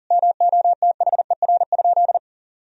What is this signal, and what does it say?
Morse code, heard as a keyed tone, about 40 wpm, 700 Hz. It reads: MOTHER?
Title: Sentences from Top 1000 Words - Review Part 4 - 40wpm